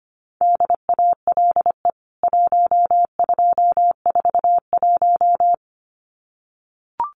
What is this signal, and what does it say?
Morse code, heard as a keyed tone, about 25 wpm, 690 Hz. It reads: DALE 1241
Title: CWT Exchange 25wpm